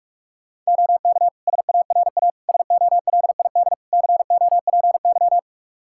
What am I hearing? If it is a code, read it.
Morse code, heard as a keyed tone, about 42 wpm, 700 Hz. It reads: OK SARA SOLID COPY
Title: Practice QSO - Exchange 3 - More Info - 45wpm